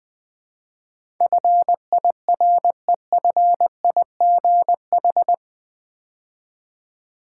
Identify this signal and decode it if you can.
Morse code, heard as a keyed tone, about 20 wpm, 700 Hz. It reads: FIREFIGH
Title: Build A Word - 4 Letters At A Time - 11-Letter Words - 20wpm